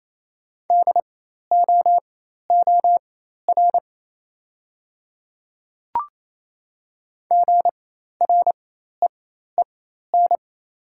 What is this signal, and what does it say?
Morse code, heard as a keyed tone, about 28 wpm, spaced as for 13 wpm, 700 Hz. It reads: DOOR GREEN
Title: 4x Character Spacing - Top 500 Words - 28wpm